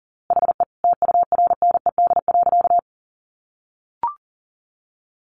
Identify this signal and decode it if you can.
Morse code, heard as a keyed tone, about 40 wpm, 700 Hz. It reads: HE TURNED.
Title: A Sound Of Thunder 40wpm